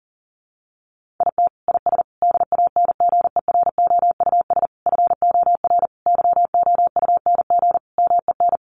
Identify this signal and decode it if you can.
Morse code, heard as a keyed tone, about 40 wpm, 700 Hz. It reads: IT IS DANGEROUS FOR YOUNG MEN
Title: Sentences from Top 1200 Words 40wpm